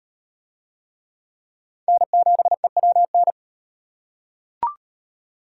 Morse code, heard as a keyed tone, about 38 wpm, 700 Hz. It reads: N7EWN